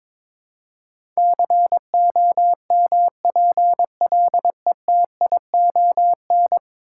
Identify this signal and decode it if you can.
Morse code, heard as a keyed tone, about 21 wpm, 700 Hz. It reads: COMPLETION
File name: practice / 22wpm / 10 Letter Words - First 3 Letters Given - 22wpm